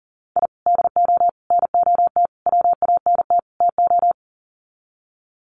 Morse code, heard as a keyed tone, about 40 wpm, 700 Hz. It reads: I DO NOT WANT TO
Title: Sets of 5 Words 40wpm